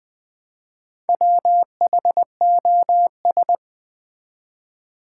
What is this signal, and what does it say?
Morse code, heard as a keyed tone, about 20 wpm, 700 Hz. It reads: WHOS
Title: Build A Word - 1 Letter At A Time - 3 to 5-Letter Words - 20wpm